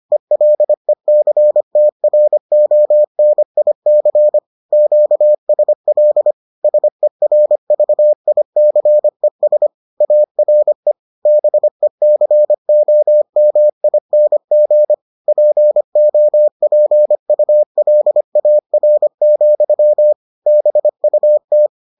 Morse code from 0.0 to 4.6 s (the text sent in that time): ELECTRONIC